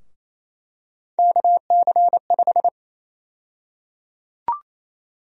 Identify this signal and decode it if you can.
Morse code, heard as a keyed tone, about 28 wpm, 700 Hz. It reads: KC5
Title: USA Call Sign Prefixes - 28wpm